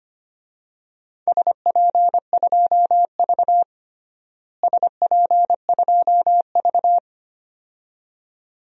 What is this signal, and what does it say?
Morse code, heard as a keyed tone, about 25 wpm, 700 Hz. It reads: SP2V SP2V
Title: DXCC-100 Call Signs - 4 Characters - Repeated 2x - 25wpm